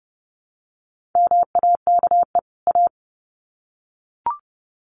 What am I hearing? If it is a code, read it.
Morse code, heard as a keyed tone, about 30 wpm, 700 Hz. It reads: MAKE A
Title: Sets of 2 Words 30wpm